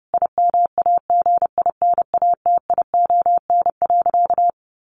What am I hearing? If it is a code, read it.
Morse code, heard as a keyed tone, about 30 wpm, 700 Hz. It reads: IMAGINATION.
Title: Sentences from Top 1300 Words 30wpm